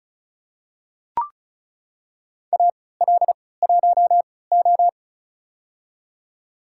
Morse code, heard as a keyed tone, about 36 wpm, spaced as for 20 wpm, 700 Hz. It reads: AL1O
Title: USA Call Signs - 4 Characters - 3x Character Spacing - 35wpm